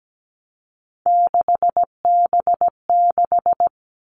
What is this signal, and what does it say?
Morse code, heard as a keyed tone, about 17 wpm, 700 Hz. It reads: 6B6
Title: All Cut Numbers - 3-Digit 17wpm